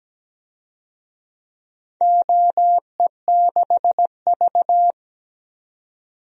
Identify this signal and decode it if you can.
Morse code, heard as a keyed tone, about 17 wpm, 700 Hz. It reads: OE6V
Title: DXCC-100 Call Signs - 4 Characters - 17wpm